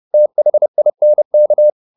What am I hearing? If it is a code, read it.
Morse code, heard as a keyed tone, about 30 wpm, 600 Hz. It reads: THINK